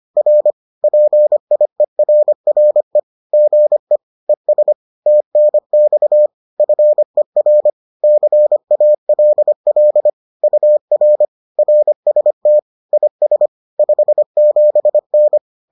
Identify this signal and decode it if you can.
Morse code, heard as a keyed tone, about 25 wpm, 600 Hz. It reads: R PIERRE GE ES TNX FER CALL UR RST IS 57N
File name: 25wpm-6